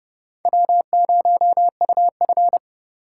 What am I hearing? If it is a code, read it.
Morse code, heard as a keyed tone, about 32 wpm, 700 Hz. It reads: W0UF
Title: K1USN SST QSO Practice - 30wpm